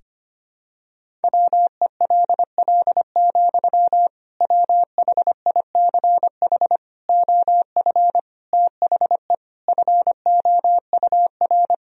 Morse code, heard as a keyed tone, about 25 wpm, 700 Hz. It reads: WELL, WHICH OF THE FOUR
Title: Sentences from Top 1000 Words - Review Part 1 - 25wpm